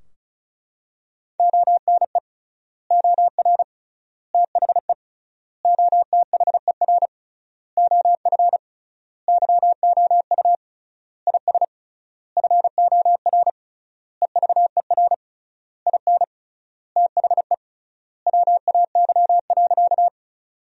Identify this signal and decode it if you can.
Morse code, heard as a keyed tone, about 36 wpm, 700 Hz. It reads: ONE OR THE OTHER OF YOU IS FOR EVER IN THE WAY.
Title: 3x Word-Spacing - Sentences from Top 400 Words - 35wpm